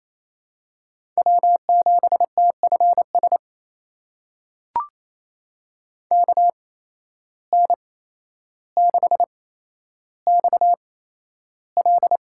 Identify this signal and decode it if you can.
Morse code, heard as a keyed tone, about 28 wpm, spaced as for 12 wpm, 700 Hz. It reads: W7TFS KN6XL
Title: USA Call Signs - 5 Characters - 8x Character Spacing - 28wpm